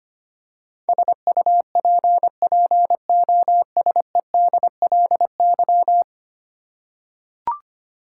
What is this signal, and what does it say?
Morse code, heard as a keyed tone, about 25 wpm, 700 Hz. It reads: SUPPOSEDLY
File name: 10 Letter Words - First 3 Letters Given - 25wpm